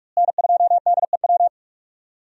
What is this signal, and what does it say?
Morse code, heard as a keyed tone, about 46 wpm, 700 Hz. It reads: N2DEW
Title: USA Call Signs - 5 Characters - Repeated 3x - 45wpm